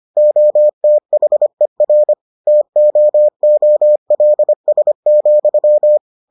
Morse code, heard as a keyed tone, about 25 wpm, 600 Hz. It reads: OTHER TOOLS,